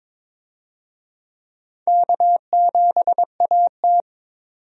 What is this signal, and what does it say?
Morse code, heard as a keyed tone, about 22 wpm, 700 Hz. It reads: K7AT
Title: USA Call Signs - 4 Characters - 22wpm